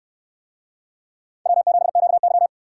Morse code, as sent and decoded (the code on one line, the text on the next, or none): .-- --... --.- -.--
W7QY